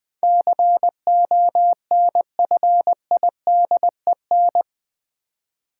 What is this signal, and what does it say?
Morse code, heard as a keyed tone, about 20 wpm, 700 Hz. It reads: CONFIDEN
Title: Build A Word - 4 Letters At A Time - 10-Letter Words - 20wpm